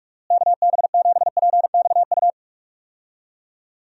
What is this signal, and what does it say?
Morse code, heard as a keyed tone, about 45 wpm, 700 Hz. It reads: KB7PXU
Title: USA Call Signs - 6 Characters - Repeated 3x - 45wpm